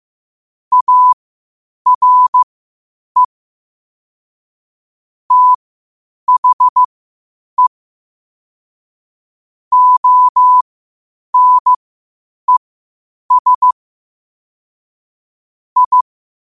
Morse code, heard as a keyed tone, about 15 wpm, spaced as for 8 wpm, 1000 Hz. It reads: ARE THE ONES I
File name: qotdfiles / 8wpm